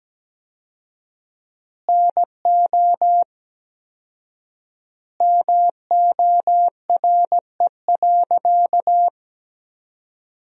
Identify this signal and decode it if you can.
Morse code, heard as a keyed tone, about 17 wpm, 700 Hz. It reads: NO MORE.
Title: 4x Word-Spacing - Sentences from Top 300 Words - 17wpm